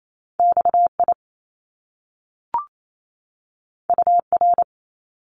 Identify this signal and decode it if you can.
Morse code, heard as a keyed tone, about 28 wpm, 700 Hz. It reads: XI UR
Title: Top Two Letter Combinations in Words 28wpm